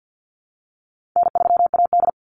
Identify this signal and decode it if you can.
Morse code, heard as a keyed tone, about 50 wpm, 700 Hz. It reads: N3AD
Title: Call Signs Repeated 3x - 50wpm